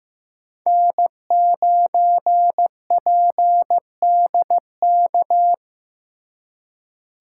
Morse code, heard as a keyed tone, about 15 wpm, 700 Hz. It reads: N9PDK
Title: USA Call Signs - 5 Characters - 2x Character Spacing - 15wpm